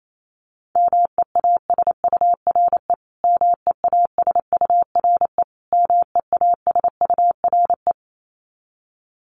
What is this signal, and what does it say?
Morse code, heard as a keyed tone, about 28 wpm, 700 Hz. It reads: MEASURE MEASURE MEASURE
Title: Top 500 Words Repeated 28wpm